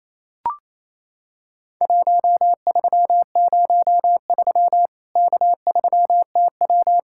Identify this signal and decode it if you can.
Morse code, heard as a keyed tone, about 28 wpm, 700 Hz. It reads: 1303 K3TW
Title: Worked All Europe DX Contest - QTC Exchange - 28wpm